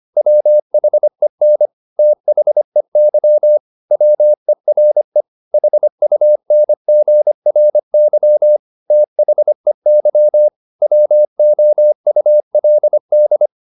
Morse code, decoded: WHEN THEY WERE HUNGRY THEY WOULD

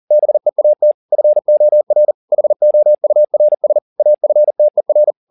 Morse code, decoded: BEAT FOR HOURS AFTER

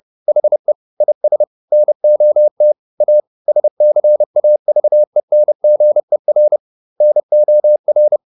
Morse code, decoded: HE IS NOT A SCAVENGER NOR

